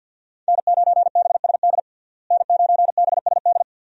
Morse code, heard as a keyed tone, about 48 wpm, 700 Hz. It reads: N9BSD N9BSD
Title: USA Call Signs - 5 Characters - Repeated 2x - 50wpm